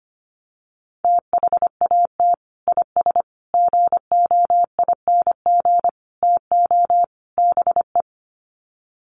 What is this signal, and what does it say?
Morse code, heard as a keyed tone, about 25 wpm, 700 Hz. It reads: THAT IS GOING TO BE
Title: Sets of 5 Words Encore2 25wpm